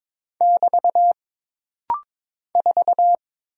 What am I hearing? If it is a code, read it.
Morse code, heard as a keyed tone, about 22 wpm, 700 Hz. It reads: <BT> 4
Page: This rapid-fire timing is geared to push your ICR (Instant Character Recognition) skills.